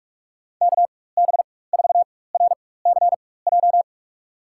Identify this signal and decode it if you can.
Morse code, heard as a keyed tone, about 44 wpm, spaced as for 21 wpm, 700 Hz. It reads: KB4RCJ
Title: USA Call Signs - 4x Character Spacing - 45wpm